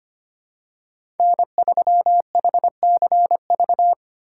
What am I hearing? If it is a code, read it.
Morse code, heard as a keyed tone, about 25 wpm, 690 Hz. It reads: N3HCV
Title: USA Call Signs - 5 Characters - 6x Character Spacing - 25wpm